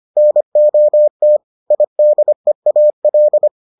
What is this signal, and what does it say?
Morse code, heard as a keyed tone, about 25 wpm, 600 Hz. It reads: NOT IDEAL